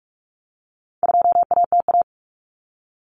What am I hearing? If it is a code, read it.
Morse code, heard as a keyed tone, about 44 wpm, 700 Hz. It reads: 2ATA
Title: Common Cut Numbers - 4-Digit 45wpm